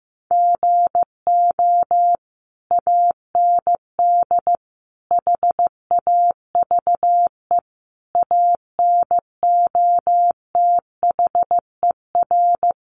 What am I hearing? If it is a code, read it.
Morse code, heard as a keyed tone, about 15 wpm, 700 Hz. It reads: GO AND HAVE ANOTHER
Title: Sentences from Top 600 Words 15wpm